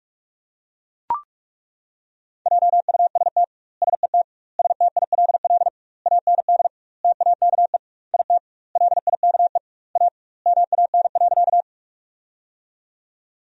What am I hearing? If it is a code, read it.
Morse code, heard as a keyed tone, about 42 wpm, 700 Hz. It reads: JUST SET STILL AND TAKE IT LIKE A MAN.
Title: 2x Word-Spacing - Sentences from Top 300 Words - 45wpm